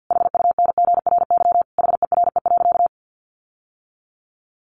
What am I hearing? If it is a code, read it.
Morse code, heard as a keyed tone, about 50 wpm, 700 Hz. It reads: HUNGRY HERE.
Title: Sentences from Top 1400 Words 50wpm